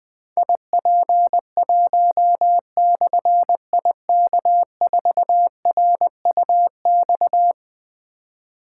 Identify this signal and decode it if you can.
Morse code, heard as a keyed tone, about 20 wpm, 700 Hz. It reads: IP1/IK4RUX
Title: Difficult Call Signs 20wpm